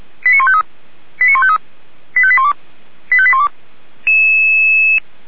Five-tone selective call (ZVEI), heard as quick runs of five tones, tones of about 70 ms. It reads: 89153 89153 85812 85812
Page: SDR/ZVEI